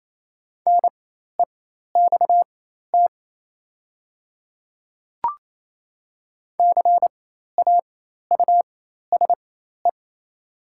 Morse code, heard as a keyed tone, about 28 wpm, spaced as for 13 wpm, 700 Hz. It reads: NEXT CAUSE
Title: 4x Character Spacing - Top 500 Words - 28wpm